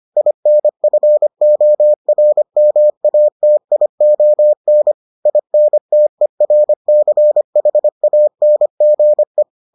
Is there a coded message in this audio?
Yes: Morse code from 0.2 to 9.4 s, heard as a keyed tone, about 25 wpm, 600 Hz. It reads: INFORMATION INTERCHANGE